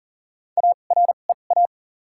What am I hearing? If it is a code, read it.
Morse code, heard as a keyed tone, about 40 wpm, spaced as for 29 wpm, 700 Hz. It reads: AREA
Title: 2x Character Spacing - Top 500 Words - 40wpm